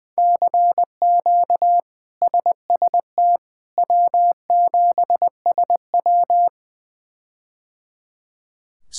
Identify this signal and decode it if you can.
Morse code, heard as a keyed tone, about 20 wpm, 700 Hz. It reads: CQ SST W7SW
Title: K1USN SST QSO Practice - 20wpm